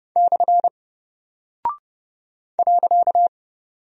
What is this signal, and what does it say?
Morse code, heard as a keyed tone, about 30 wpm, 700 Hz. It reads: / .
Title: Single Character (Letter:Number:Punctuation:Prosign) 30wpm